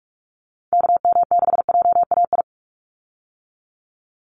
Morse code, heard as a keyed tone, about 44 wpm, 700 Hz. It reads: KM6JAI
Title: USA Call Signs 45wpm